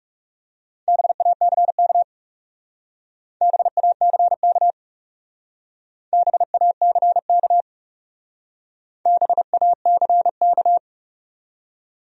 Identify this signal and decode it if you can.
Morse code, heard as a keyed tone, about 36 wpm, 700 Hz. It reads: BACK BACK BACK BACK
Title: Speed-Racing Top 300 Words 30wpm